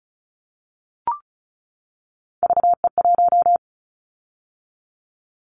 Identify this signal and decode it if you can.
Morse code, heard as a keyed tone, about 34 wpm, 700 Hz. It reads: VE1
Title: Callsign Prefixes for Top 100 DXCC - 35wpm